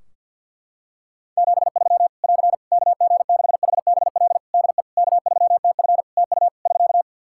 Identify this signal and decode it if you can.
Morse code, heard as a keyed tone, about 50 wpm, 700 Hz. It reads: 73 <AR> KG6HBL DE K3TF TU <SK>
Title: Practice QSO - Exchange 4 - Ending - 50wpm